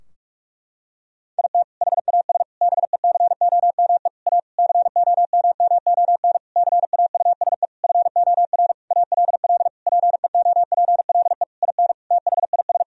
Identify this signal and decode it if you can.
Morse code, heard as a keyed tone, about 42 wpm, 700 Hz. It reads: IT HAS BECOME A COMMON CAUSE FOR ALL PEOPLE IN THIS
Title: Sentences from Top 1000 Words - Review Part 3 - 45wpm